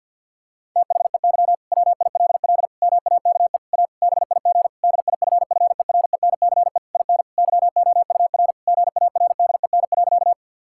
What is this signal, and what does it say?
Morse code, heard as a keyed tone, about 50 wpm, 700 Hz. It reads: THEY WILL MAKE A BIG DIFFERENCE IN YOUR GARDEN.
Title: Sentences from Top 1000 Words - Review Part 5 - 50wpm